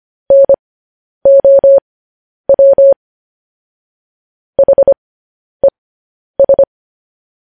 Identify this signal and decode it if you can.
Morse code, heard as a keyed tone, about 25 wpm, spaced as for 10 wpm, 550 Hz. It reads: NOW HES
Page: news-headlines-in-morse-code-at-10-wpm